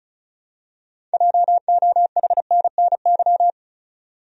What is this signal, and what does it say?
Morse code, heard as a keyed tone, about 34 wpm, 700 Hz. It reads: JOHNNY
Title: USA Common Names - 35wpm